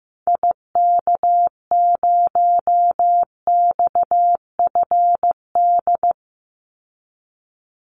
Morse code, heard as a keyed tone, about 15 wpm, 700 Hz. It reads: IK0XFD
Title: Call Signs Repeated 3x - 15wpm